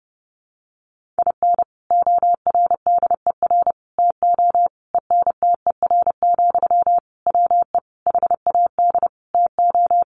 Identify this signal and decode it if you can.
Morse code, heard as a keyed tone, about 30 wpm, 700 Hz. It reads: IN ORDER TO ENTER, WE HAD TO
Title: Sentences from Top 1300 Words 30wpm